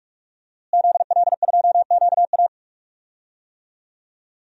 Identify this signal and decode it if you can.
Morse code, heard as a keyed tone, about 42 wpm, 700 Hz. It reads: ZL2QA
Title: DXCC-100 Call Signs - 5 Characters - Repeated 2x - 45wpm